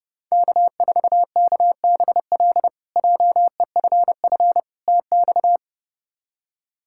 Morse code, heard as a keyed tone, about 32 wpm, 700 Hz. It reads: K4KBL JEFF TX
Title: K1USN SST QSO Practice - 30wpm